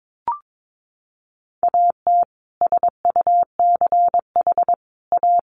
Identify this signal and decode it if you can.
Morse code, heard as a keyed tone, about 22 wpm, 700 Hz. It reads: AT SUCH A
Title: Sentences from Top 1000 Words 22wpm